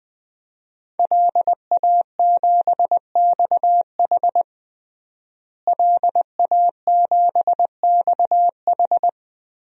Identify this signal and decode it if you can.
Morse code, heard as a keyed tone, about 20 wpm, 700 Hz. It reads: LA7XH LA7XH
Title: DXCC-100 Call Signs - 5 Characters - Repeated 3x - 20wpm